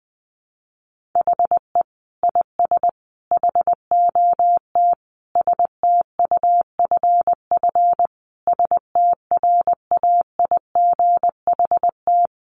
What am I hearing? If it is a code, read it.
Morse code, heard as a keyed tone, about 20 wpm, 700 Hz. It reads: HE IS HOT STUFF STRAIGHT
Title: Sentences from Top 1500 Words 20wpm